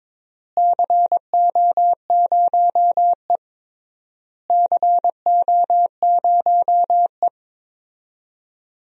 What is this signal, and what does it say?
Morse code, heard as a keyed tone, about 22 wpm, 700 Hz. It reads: CO0E CO0E
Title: DXCC-100 Call Signs - 4 Characters - Repeated 3x - 22wpm